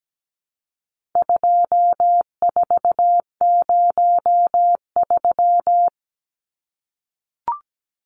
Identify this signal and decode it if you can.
Morse code, heard as a keyed tone, about 17 wpm, 700 Hz. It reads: 2403
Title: Numbers - 4-Digit 17wpm